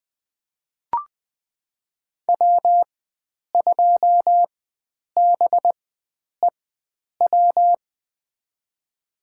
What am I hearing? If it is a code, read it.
Morse code, heard as a keyed tone, about 21 wpm, spaced as for 9 wpm, 700 Hz. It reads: W2BEW
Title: USA Call Signs - 5 Characters - 4x Character Spacing - 20wpm